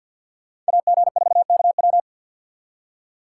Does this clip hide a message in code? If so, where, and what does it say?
Yes: Morse code from 0.7 to 2.0 s, heard as a keyed tone, about 64 wpm, 700 Hz. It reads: AG4KW